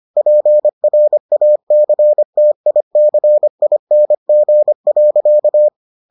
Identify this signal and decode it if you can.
Morse code, heard as a keyed tone, about 25 wpm, 600 Hz. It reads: PRACTICING.